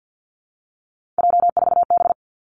Morse code, heard as a keyed tone, about 50 wpm, 700 Hz. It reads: J4B
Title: Three Challenging Letter-Number Combinations 50wpm